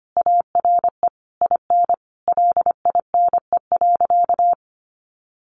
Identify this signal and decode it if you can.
Morse code, heard as a keyed tone, about 25 wpm, 700 Hz. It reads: ARE IN LINE.
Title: Additional Easy Phrases 25wpm